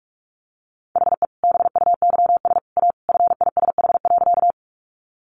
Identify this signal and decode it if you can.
Morse code, heard as a keyed tone, about 44 wpm, 700 Hz. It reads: HE BUYS A FISH.
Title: Easy Phrases 45wpm